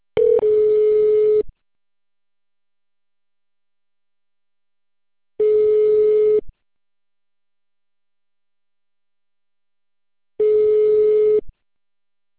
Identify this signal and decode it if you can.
Ringback tone (Europe, 425 Hz).